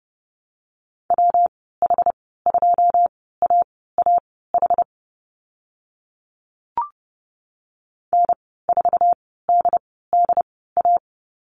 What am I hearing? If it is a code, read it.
Morse code, heard as a keyed tone, about 30 wpm, spaced as for 17 wpm, 700 Hz. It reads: WH2AAH N4DDA
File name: Lesson 19g - Letter w - Call Signs 30wpm